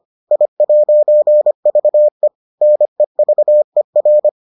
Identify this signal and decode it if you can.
Morse code, heard as a keyed tone, about 25 wpm, 600 Hz. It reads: I'VE NEVER